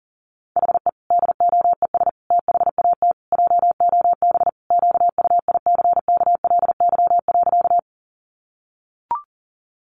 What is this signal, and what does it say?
Morse code, heard as a keyed tone, about 37 wpm, 700 Hz. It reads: HE DOES THAT JOB QUICKLY.
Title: Sentences from Top 700 Words 40wpm